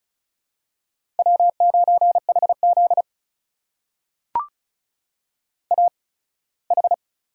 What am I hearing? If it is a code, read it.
Morse code, heard as a keyed tone, about 36 wpm, spaced as for 21 wpm, 700 Hz. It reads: W9HZ AH
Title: USA Call Signs - 4 Characters - 8x Character Spacing - 35wpm